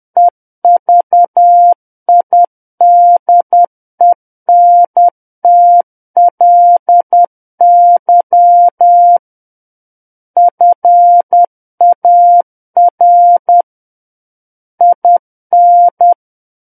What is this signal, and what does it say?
Morse code, heard as a keyed tone, about 10 wpm, 700 Hz. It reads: EVIDENTLY FAR IN
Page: War of the Worlds - 01-Chapter 1 - 10 WPM